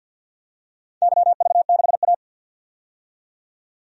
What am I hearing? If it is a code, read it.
Morse code, heard as a keyed tone, about 51 wpm, 700 Hz. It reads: YV6A
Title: DXCC-100 Call Signs - 4 Characters - 50wpm